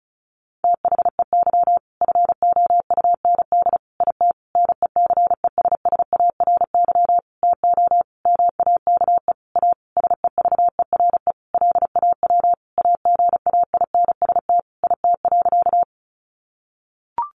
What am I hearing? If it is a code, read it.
Morse code, heard as a keyed tone, about 34 wpm, 700 Hz. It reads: THEY FOUND IT NECESSARY TO MAKE A SEVERE LAW AGAINST IT.
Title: Sentences from Top 1400 Words 35wpm